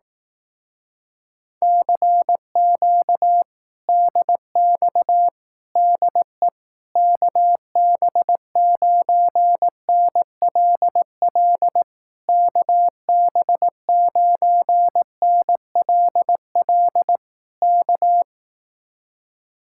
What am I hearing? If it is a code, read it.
Morse code, heard as a keyed tone, about 18 wpm, 700 Hz. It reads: CQ DX DE KB9NLL KB9NLL K
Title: Speed-Racing Common QSO Phrases Encore - 15wpm